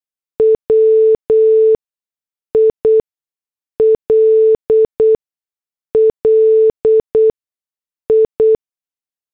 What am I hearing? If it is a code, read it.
Morse code, heard as a keyed tone, about 8 wpm, spaced as for 6 wpm, 440 Hz. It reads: WILL I
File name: morse / headlines